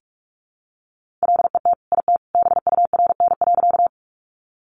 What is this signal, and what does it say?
Morse code, heard as a keyed tone, about 44 wpm, 700 Hz. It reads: LET IT BURN.
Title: Sentences from Top 1300 Words 45wpm